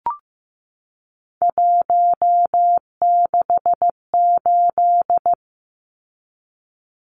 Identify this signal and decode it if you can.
Morse code, heard as a keyed tone, about 15 wpm, 700 Hz. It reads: 168
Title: Numbers - 3-Digit 15wpm